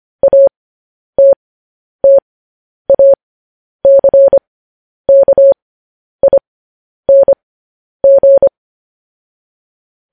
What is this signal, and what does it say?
Morse code, heard as a keyed tone, about 25 wpm, spaced as for 10 wpm, 550 Hz. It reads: ATTACKING
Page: news headlines in morse code at 10 wpm